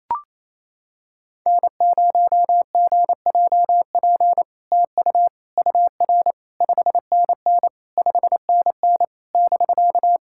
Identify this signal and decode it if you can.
Morse code, heard as a keyed tone, about 28 wpm, 700 Hz. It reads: N0GJP TU UR 5NN 5NN <BK>
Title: SOTA QSO Practice 28wpm